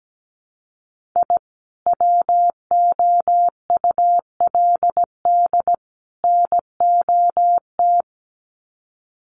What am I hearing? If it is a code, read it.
Morse code, heard as a keyed tone, about 17 wpm, 700 Hz. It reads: I WOULD NOT
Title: Sets of 3 Words 17wpm